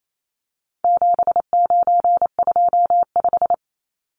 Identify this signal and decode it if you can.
Morse code, heard as a keyed tone, about 28 wpm, 700 Hz. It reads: 7925
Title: Numbers - 4-Digit 28wpm